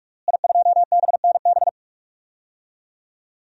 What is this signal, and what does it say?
Morse code, heard as a keyed tone, about 44 wpm, 700 Hz. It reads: I2BNB
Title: DXCC-100 Call Signs - 5 Characters - Repeated 2x - 45wpm